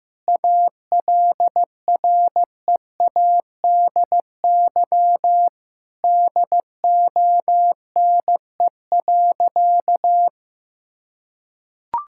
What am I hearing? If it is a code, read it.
Morse code, heard as a keyed tone, about 15 wpm, 700 Hz. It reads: ALREADY DONE.
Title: Sentences from Top 1600 Words 15wpm